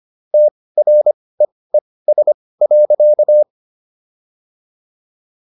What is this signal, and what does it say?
Morse code, heard as a keyed tone, about 25 wpm, spaced as for 18 wpm, 600 Hz. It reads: TREES.